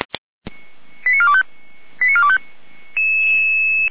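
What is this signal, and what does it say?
Five-tone selective call (ZVEI), heard as quick runs of five tones, tones of about 70 ms. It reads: 89425 89425
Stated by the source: ZVEI five-tone selective call received by SDR.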